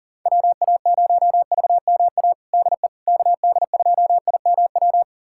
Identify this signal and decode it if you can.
Morse code, heard as a keyed tone, about 40 wpm, 700 Hz. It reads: WA0VMA DE KD2IMW
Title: USA Callsign DE Callsign 40wpm